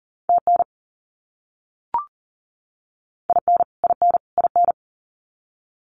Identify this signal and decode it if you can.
Morse code, heard as a keyed tone, about 40 wpm, 700 Hz. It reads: TN IN IN IN
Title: US States Repeated 3x 40wpm